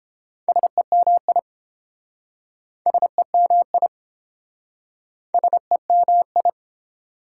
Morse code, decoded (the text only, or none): SEMI SEMI SEMI